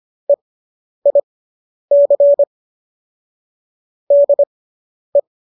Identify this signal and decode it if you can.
Morse code, heard as a keyed tone, about 25 wpm, spaced as for 10 wpm, 570 Hz. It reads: EIC DE